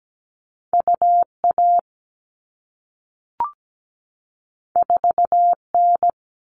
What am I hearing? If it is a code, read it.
Morse code, heard as a keyed tone, about 17 wpm, 700 Hz. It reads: UA 4N
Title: All Cut Numbers - 2-Digit 17wpm